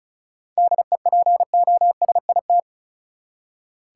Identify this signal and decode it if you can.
Morse code, heard as a keyed tone, about 36 wpm, 700 Hz. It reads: DEPOSIT
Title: Seven Letter Words - 35wpm